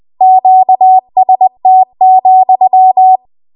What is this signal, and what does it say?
Morse code, heard as a keyed tone, about 20 wpm, 750 Hz. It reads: QST,